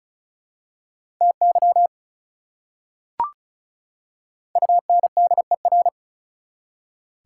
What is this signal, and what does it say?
Morse code, heard as a keyed tone, about 36 wpm, 690 Hz. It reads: TY UNDER
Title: Prefixes and Suffixes 35wpm